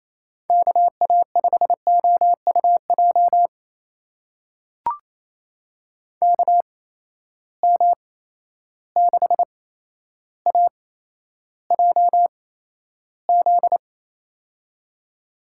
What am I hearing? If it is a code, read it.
Morse code, heard as a keyed tone, about 28 wpm, spaced as for 12 wpm, 700 Hz. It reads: KA5OUJ KM6AJZ
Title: USA Call Signs - 6 Characters - 8x Character Spacing - 28wpm